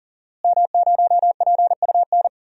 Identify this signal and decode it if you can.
Morse code, heard as a keyed tone, about 40 wpm, 700 Hz. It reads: M0PUN
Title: DXCC-100 Call Signs - 5 Characters - Repeated 2x - 40wpm